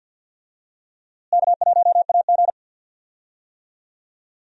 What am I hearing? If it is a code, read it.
Morse code, heard as a keyed tone, about 51 wpm, 700 Hz. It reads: K1AG
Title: USA Call Signs - 4 Characters - Repeated 3x - 50wpm